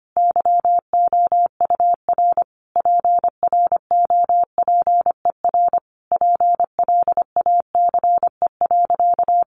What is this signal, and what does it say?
Morse code, heard as a keyed tone, about 25 wpm, 700 Hz. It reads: YOUR PROPER PLACE.
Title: Sentences from Top 800 Words 25wpm